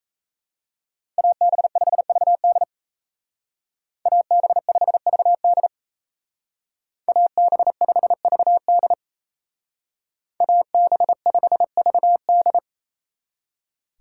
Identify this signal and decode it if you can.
Morse code, heard as a keyed tone, about 34 wpm, 700 Hz. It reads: AB5VD AB5VD AB5VD AB5VD
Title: Speed Racing - USA 2x2 Call Signs 28wpm